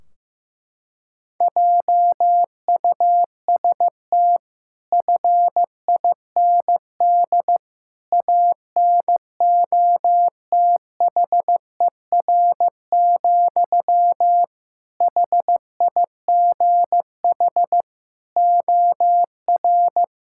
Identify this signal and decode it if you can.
Morse code, heard as a keyed tone, about 15 wpm, 700 Hz. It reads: JUST FIND ANOTHER, HIGH OR
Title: Sentences from Top 1600 Words 15wpm